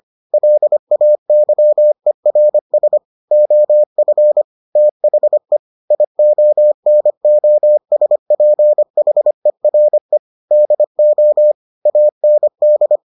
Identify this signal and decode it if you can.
Morse code, heard as a keyed tone, about 25 wpm, 600 Hz. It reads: LAYERS OF THE IONOSPHERE DO AND